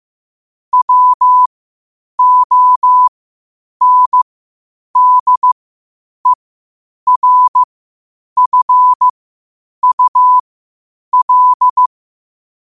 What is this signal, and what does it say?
Morse code, heard as a keyed tone, about 15 wpm, spaced as for 8 wpm, 1000 Hz. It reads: WONDERFUL
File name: qotdfiles / 8wpm